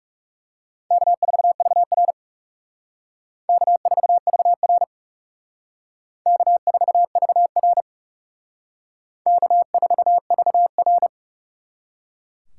Morse code, heard as a keyed tone, about 37 wpm, 700 Hz. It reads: K4VR K4VR K4VR K4VR
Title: Speed Racing - USA Call Signs 30wpm